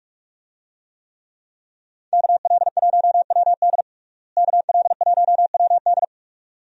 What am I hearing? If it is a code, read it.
Morse code, heard as a keyed tone, about 41 wpm, 700 Hz. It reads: KL1WD KL1WD
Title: USA Call Signs - 5 Characters - Repeated 2x - 45wpm